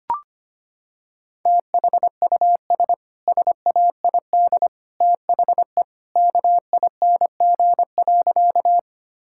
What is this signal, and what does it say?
Morse code, heard as a keyed tone, about 25 wpm, 700 Hz. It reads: THUS SAID THE KING.
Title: Sentences from Top 1000 Words - Review Part 5 - 25wpm